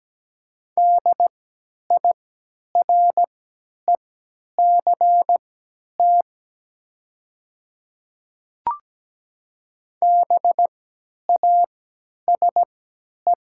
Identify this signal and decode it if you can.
Morse code, heard as a keyed tone, about 17 wpm, spaced as for 10 wpm, 700 Hz. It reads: DIRECT BASE
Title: 3x Character Spacing - Top 500 Words - 17wpm